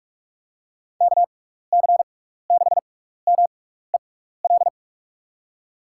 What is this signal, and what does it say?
Morse code, heard as a keyed tone, about 46 wpm, spaced as for 16 wpm, 700 Hz. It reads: KC6MEL
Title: USA Call Signs - 6 Characters - 6x Character Spacing - 45wpm